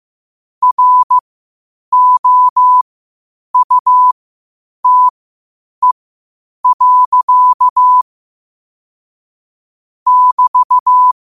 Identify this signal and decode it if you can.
Morse code, heard as a keyed tone, about 15 wpm, spaced as for 8 wpm, 1000 Hz. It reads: ROUTE. <BT>
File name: qotdfiles / 8wpm